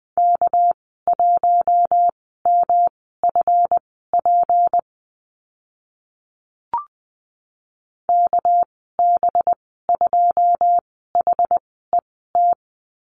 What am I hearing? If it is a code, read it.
Morse code, heard as a keyed tone, about 20 wpm, spaced as for 14 wpm, 700 Hz. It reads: K1MFP KB2HET
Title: 231 Lesson 37g - Letter x - Call Signs 20wpm